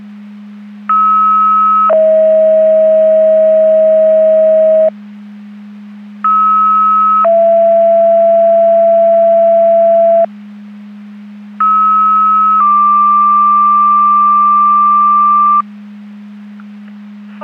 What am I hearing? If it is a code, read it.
Two-tone page, heard as a one-second tone, then a three-second tone, about 1290 then 640 Hz, about 1290 then 670 Hz, about 1290 then 1180 Hz. It is heard from 0.9 to 15.6 s.
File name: Portage / 03331 07-37 Portage 154.385 Paging